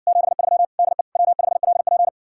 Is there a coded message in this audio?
Yes: Morse code from 0.1 to 2.1 s, heard as a keyed tone, about 61 wpm, 700 Hz. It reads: 73 DE W5LP